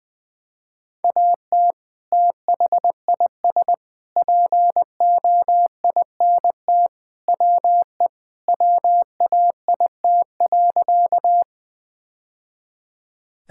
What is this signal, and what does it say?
Morse code, heard as a keyed tone, about 20 wpm, 700 Hz. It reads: AT THIS POINT WE WAIT.
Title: Sentences with 3-Word Phrases - Spoken Beginning - 20wpm